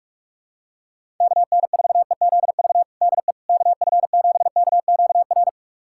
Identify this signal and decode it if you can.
Morse code, heard as a keyed tone, about 41 wpm, 700 Hz. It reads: KN4EZV DE KF7KQR
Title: USA Callsign DE Callsign 45wpm